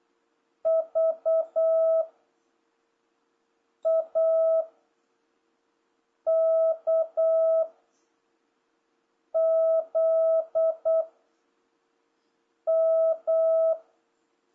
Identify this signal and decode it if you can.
Morse code, heard as a keyed tone, about 8 wpm, 660 Hz. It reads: V A K Z M